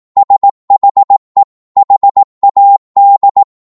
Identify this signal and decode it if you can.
Morse code, heard as a keyed tone, about 18 wpm, 800 Hz. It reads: SHE HAD
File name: 18wpm